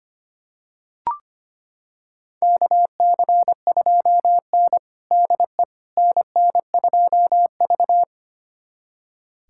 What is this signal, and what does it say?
Morse code, heard as a keyed tone, about 25 wpm, 700 Hz. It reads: KC2N DE NN2V
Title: USA Callsign DE Callsign - 4 Characters - 25wpm